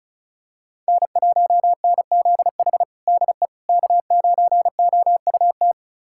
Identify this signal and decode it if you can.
Morse code, heard as a keyed tone, about 34 wpm, 700 Hz. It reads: N1NZH DE K9OUT
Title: USA Callsign DE Callsign - 5 Characters - 35wpm